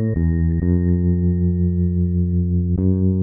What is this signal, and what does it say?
Dial tone (North America), 350 + 440 Hz.